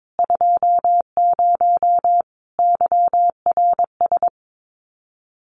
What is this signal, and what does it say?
Morse code, heard as a keyed tone, about 22 wpm, 700 Hz. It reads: 20 YRS
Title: Common QSO Phrases 22wpm